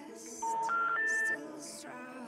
Special information tone.